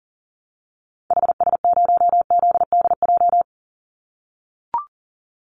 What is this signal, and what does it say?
Morse code, heard as a keyed tone, about 38 wpm, 700 Hz. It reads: HS0ZDJ
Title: Call Signs - 40wpm